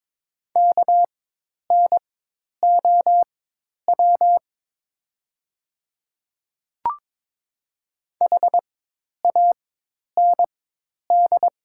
Morse code, heard as a keyed tone, about 22 wpm, spaced as for 10 wpm, 690 Hz. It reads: KNOW HAND
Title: 4x Character Spacing - Top 300 Words - 22wpm